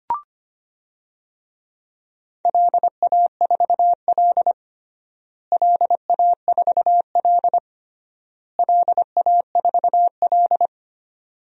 Morse code, heard as a keyed tone, about 25 wpm, 700 Hz. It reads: LA4L LA4L LA4L
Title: DXCC-100 Call Signs - 4 Characters - Repeated 3x - 25wpm